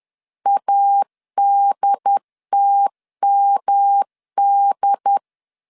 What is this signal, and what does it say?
Morse code, heard as a keyed tone, about 10 wpm, 800 Hz. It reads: ADTMD
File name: CW / DET / 1 3 1500C S B 10wpm 3335